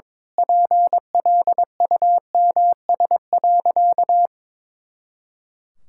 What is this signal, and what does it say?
Morse code, heard as a keyed tone, about 22 wpm, 700 Hz. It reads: PLUMS.
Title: Pangram Challenge v2 - 22wpm